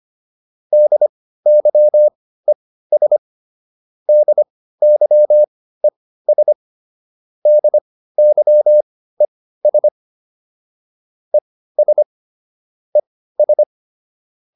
Morse code, heard as a keyed tone, about 25 wpm, spaced as for 15 wpm, 600 Hz. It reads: DYES DYES DYES ES ES